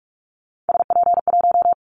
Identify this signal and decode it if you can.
Morse code, heard as a keyed tone, about 46 wpm, 700 Hz. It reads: SP1
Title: Callsign Prefixes for Top 25 DXCC - 45wpm